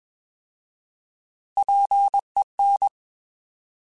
Morse code, heard as a keyed tone, about 21 wpm, 800 Hz. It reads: PEN
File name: Alphabetical / Alphabetical mp3s / Ps @20